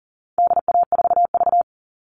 Morse code, heard as a keyed tone, about 40 wpm, 700 Hz. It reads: DA4V
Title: All Cut Numbers - 4-Digit 40wpm